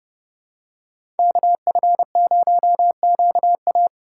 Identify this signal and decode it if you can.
Morse code, heard as a keyed tone, about 30 wpm, 700 Hz. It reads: KF0QA
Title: USA Call Signs - 5 Characters - 2x Character Spacing - 30wpm